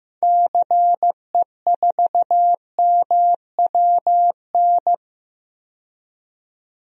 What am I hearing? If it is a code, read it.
Morse code, heard as a keyed tone, about 15 wpm, 700 Hz. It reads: CE4MWN
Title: DXCC-100 Call Signs - 6 Characters - 15wpm